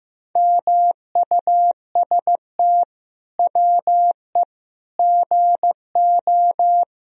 Morse code, heard as a keyed tone, about 16 wpm, 700 Hz. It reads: MUST WE GO
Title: Sentences from Top 1000 Words - Review Part 3 - 15wpm